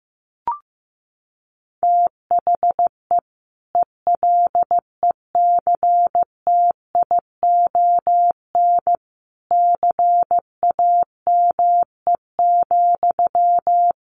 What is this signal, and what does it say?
Morse code, heard as a keyed tone, about 15 wpm, 700 Hz. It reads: THE ELECTION CAME,
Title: Sentences from Top 1500 Words 15wpm